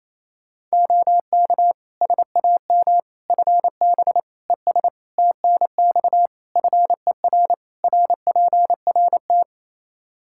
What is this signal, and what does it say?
Morse code, heard as a keyed tone, about 28 wpm, 700 Hz. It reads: OK SAM FB ES TNX FER RPRT
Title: Practice QSO - Exchange 2 - Info - 28wpm